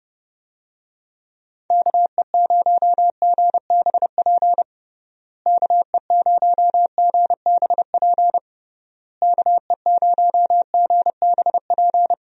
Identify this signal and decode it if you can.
Morse code, heard as a keyed tone, about 32 wpm, 700 Hz. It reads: KE0GBP KE0GBP KE0GBP
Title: USA Call Signs - 6 Characters - Repeated 3x - 30wpm